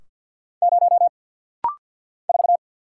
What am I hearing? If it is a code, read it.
Morse code, heard as a keyed tone, about 49 wpm, 700 Hz. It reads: 0 4
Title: Single Letter-Number - Rapid-Fire 50wpm